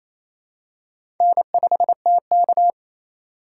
Morse code, heard as a keyed tone, about 28 wpm, 700 Hz. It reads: N5TK
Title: USA Call Signs - 4 Characters - 2x Character Spacing - 28wpm